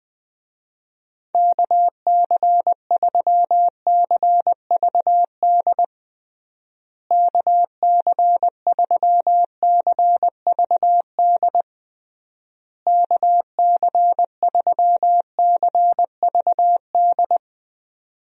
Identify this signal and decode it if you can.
Morse code, heard as a keyed tone, about 21 wpm, 700 Hz. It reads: KC3CVD KC3CVD KC3CVD
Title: USA Call Signs - 6 Characters - Repeated 3x - 20wpm